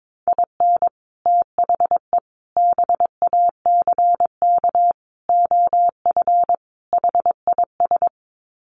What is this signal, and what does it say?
Morse code, heard as a keyed tone, about 22 wpm, 700 Hz. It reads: IN THE BACK OF HIS
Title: Sets of 5 Words Encore2 22wpm